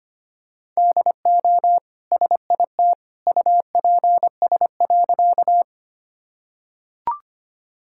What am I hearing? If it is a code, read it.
Morse code, heard as a keyed tone, about 25 wpm, 700 Hz. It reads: DO SIT UPS.
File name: Easy Sentences with Hints - 25wpm